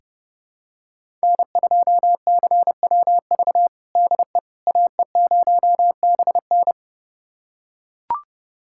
Morse code, heard as a keyed tone, about 32 wpm, 700 Hz. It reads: N2CWV DE AE0BN
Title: USA Callsign DE Callsign - 5 Characters - 30wpm